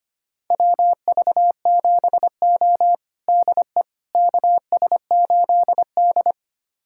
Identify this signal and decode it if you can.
Morse code, heard as a keyed tone, about 25 wpm, 700 Hz. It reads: WV7O DE KS8D
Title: USA Callsign DE Callsign - 4 Characters - 25wpm